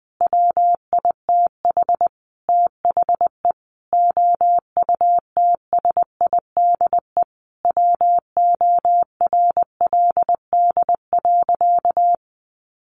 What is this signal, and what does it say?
Morse code, heard as a keyed tone, about 20 wpm, 700 Hz. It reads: WITH THE OUTSIDE WORLD.
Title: Sentences from Top 1300 Words 20wpm